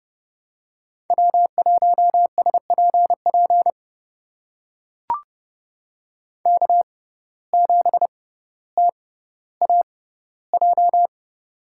Morse code, heard as a keyed tone, about 29 wpm, spaced as for 15 wpm, 700 Hz. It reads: W1SPP K7TAJ
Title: USA Call Signs - 5 Characters - 6x Character Spacing - 30wpm